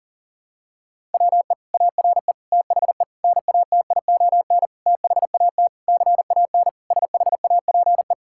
Morse code, heard as a keyed tone, about 40 wpm, 700 Hz. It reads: WE ARE THE NATION THAT CAN SHAPE
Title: Sentences from Top 1000 Words - Review Part 6 - 40wpm